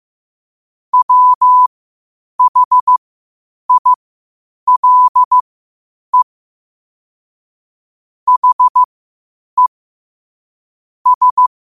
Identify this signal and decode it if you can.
Morse code, heard as a keyed tone, about 15 wpm, spaced as for 8 wpm, 1000 Hz. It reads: WHILE HE S
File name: qotdfiles / 8wpm